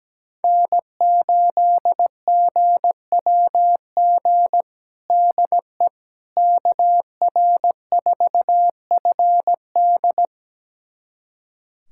Morse code, heard as a keyed tone, about 17 wpm, 700 Hz. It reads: N8GWG DE KR4FD
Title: USA Callsign DE Callsign - 5 Characters - 17wpm